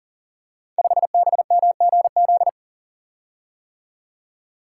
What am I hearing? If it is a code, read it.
Morse code, heard as a keyed tone, about 39 wpm, 700 Hz. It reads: 5BMGZ
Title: Code Groups - 5 Letter:Numbers - 40wpm